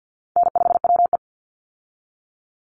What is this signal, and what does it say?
Morse code, heard as a keyed tone, about 50 wpm, 700 Hz. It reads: N5WE
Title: Easy Call Signs Repeated 2x - 50wpm